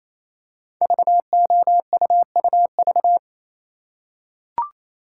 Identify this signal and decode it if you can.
Morse code, heard as a keyed tone, about 27 wpm, 700 Hz. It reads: VOUUV
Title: Code Groups - 5 Letters - 28wpm